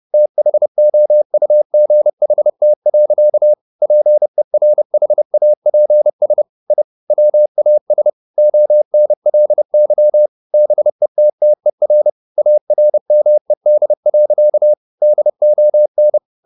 Morse code, heard as a keyed tone, about 30 wpm, 600 Hz. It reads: THOUGHT. PERHAPS I WAS ONLY BETTER ARMED. DON